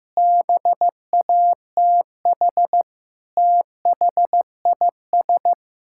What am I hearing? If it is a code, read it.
Morse code, heard as a keyed tone, about 15 wpm, 700 Hz. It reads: BATH THIS
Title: Sentences from Top 1600 Words 15wpm